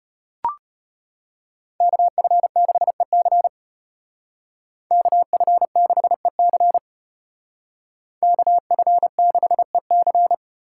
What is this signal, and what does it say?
Morse code, heard as a keyed tone, about 30 wpm, 700 Hz. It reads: KF6EC KF6EC KF6EC
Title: Speed Racing - USA Call Signs 25wpm